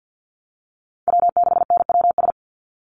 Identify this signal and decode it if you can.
Morse code, heard as a keyed tone, about 48 wpm, 700 Hz. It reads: W6NWS
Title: Call Signs Repeated 3x - 50wpm